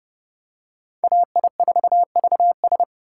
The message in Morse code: AI4VS

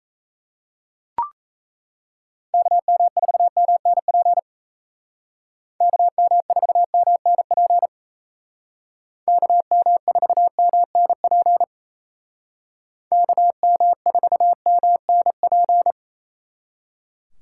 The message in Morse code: KM4MNP KM4MNP KM4MNP KM4MNP

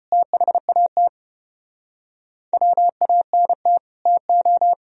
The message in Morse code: THAT WANT TO

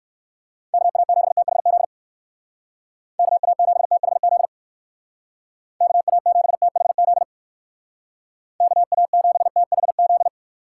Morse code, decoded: KA7THZ KA7THZ KA7THZ KA7THZ

